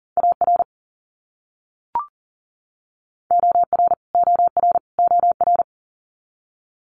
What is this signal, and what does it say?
Morse code, heard as a keyed tone, about 40 wpm, 700 Hz. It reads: AR OR OR OR
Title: US States Repeated 3x 40wpm